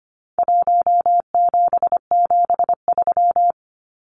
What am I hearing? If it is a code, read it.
Morse code, heard as a keyed tone, about 25 wpm, 700 Hz. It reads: 1773
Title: Numbers - Contest Style - 4-Digit 25wpm